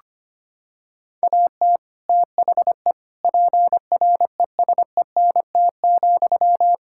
Morse code, heard as a keyed tone, about 25 wpm, 700 Hz. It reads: AT THE PRESENT,
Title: Sentences from Top 1000 Words - Review Part 5 - 25wpm